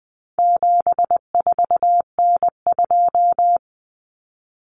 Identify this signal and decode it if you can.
Morse code, heard as a keyed tone, about 20 wpm, 700 Hz. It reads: 74N2
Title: Common Cut Numbers - 4-Digit 20wpm